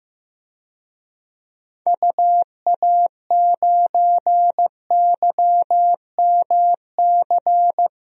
Morse code, heard as a keyed tone, about 15 wpm, 700 Hz. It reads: UA9YMC
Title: DXCC-100 Call Signs - 6 Characters - Repeated 2x - 15wpm